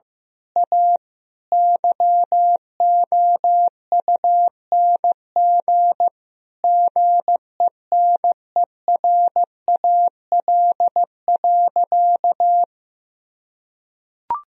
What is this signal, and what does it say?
Morse code, heard as a keyed tone, about 15 wpm, 700 Hz. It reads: A YOUNG GENERAL.
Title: Sentences from Top 1000 Words - Review Part 3 - 15wpm